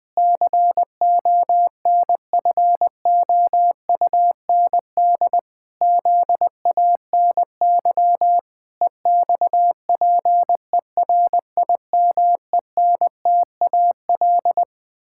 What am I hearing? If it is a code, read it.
Morse code, heard as a keyed tone, about 20 wpm, 700 Hz. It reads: CONFOUND ZANY EXPERIMENTAL
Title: Pangram Challenge v2 - 20wpm